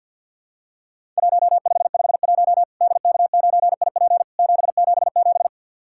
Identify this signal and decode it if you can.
Morse code, heard as a keyed tone, about 50 wpm, 700 Hz. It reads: 1551 DK9IP 777
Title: Worked All Europe DX Contest - QTC Exchange with Repeats - 50wpm